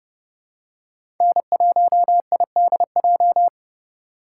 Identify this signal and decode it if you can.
Morse code, heard as a keyed tone, about 30 wpm, 700 Hz. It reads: N1IDJ
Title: USA Call Signs - 3x Character Spacing - 30wpm